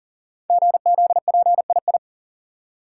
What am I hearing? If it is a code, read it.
Morse code, heard as a keyed tone, about 40 wpm, 700 Hz. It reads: GZPII
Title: Code Groups - 5 Letter:Numbers - 40wpm